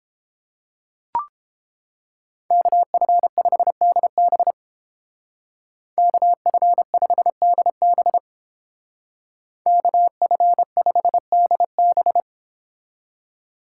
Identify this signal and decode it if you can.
Morse code, heard as a keyed tone, about 30 wpm, 700 Hz. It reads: KF5DB KF5DB KF5DB